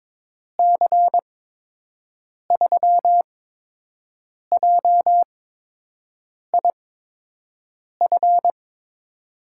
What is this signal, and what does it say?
Morse code, heard as a keyed tone, about 22 wpm, spaced as for 6 wpm, 700 Hz. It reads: C3JIF
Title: USA Call Signs - 6 Characters - 8x Character Spacing - 22wpm